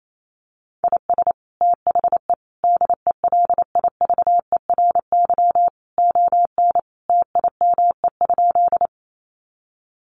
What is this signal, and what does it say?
Morse code, heard as a keyed tone, about 28 wpm, 700 Hz. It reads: IS THE DELIVERY ON TIME?
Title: Sentences from Top 1500 Words 28wpm